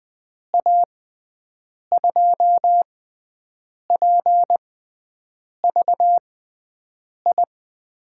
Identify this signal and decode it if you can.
Morse code, heard as a keyed tone, about 20 wpm, spaced as for 7 wpm, 700 Hz. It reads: A2PVI
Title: USA Call Signs - 6 Characters - 6x Character Spacing - 20wpm